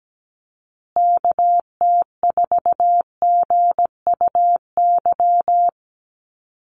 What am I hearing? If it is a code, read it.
Morse code, heard as a keyed tone, about 17 wpm, 700 Hz. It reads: KT4GUY
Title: USA Call Signs Repeated 2x 17wpm